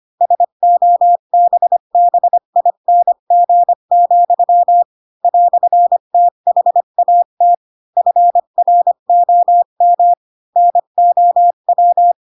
Morse code, heard as a keyed tone, about 25 wpm, 700 Hz. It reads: SOBBING, "THAT FROM NOW
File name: Pinocchio / Pinocchio ch8 20 25 ik1hge